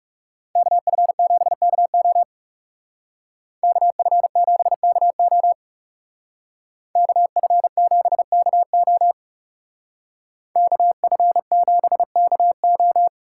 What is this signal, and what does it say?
Morse code, heard as a keyed tone, about 35 wpm, 700 Hz. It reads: KF7KO KF7KO KF7KO KF7KO
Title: Speed Racing - USA 2x2 Call Signs 30wpm